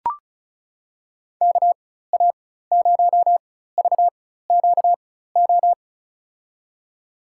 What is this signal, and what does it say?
Morse code, heard as a keyed tone, about 36 wpm, spaced as for 16 wpm, 700 Hz. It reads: KA0VQO
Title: USA Call Signs - 6 Characters - 4x Character Spacing - 35wpm